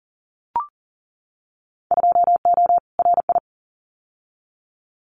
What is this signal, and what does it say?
Morse code, heard as a keyed tone, about 40 wpm, 700 Hz. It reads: 2O RI
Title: Winter Field Day Exchanges 40wpm